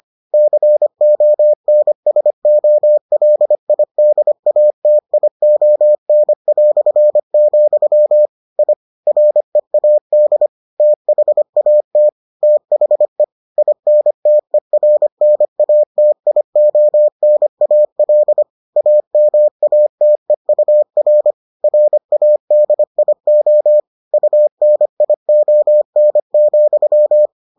Morse code, heard as a keyed tone, about 25 wpm, 600 Hz. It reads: CONSOLIDATION", I READ THAT THE INTERNATIONAL AMATEUR RADIO UNION,